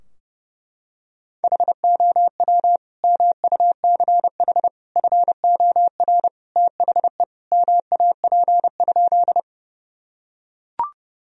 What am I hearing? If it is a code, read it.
Morse code, heard as a keyed tone, about 30 wpm, 700 Hz. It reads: HOW MUCH FOR THE MAP?
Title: Sentences from Top 1000 Words - Review Part 2 - 30wpm